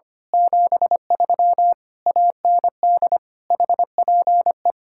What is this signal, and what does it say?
Morse code, heard as a keyed tone, about 24 wpm, 700 Hz. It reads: 73 AND HPE
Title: Speed-Racing Common QSO Phrases - 25wpm